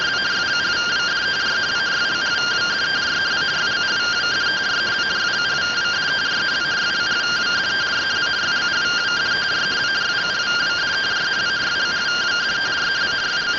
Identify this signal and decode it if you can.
Two-tone teleprinter signal, about 170 Hz shift.